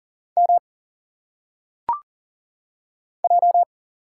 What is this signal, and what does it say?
Morse code, heard as a keyed tone, about 40 wpm, 700 Hz. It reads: M J
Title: Single Letters - Phonetic Pronunciation 40wpm